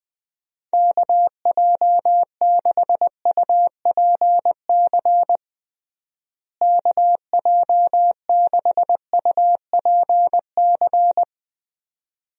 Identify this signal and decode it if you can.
Morse code, heard as a keyed tone, about 20 wpm, 700 Hz. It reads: KJ6UPC KJ6UPC
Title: USA Call Signs - 6 Characters - Repeated 2x - 20wpm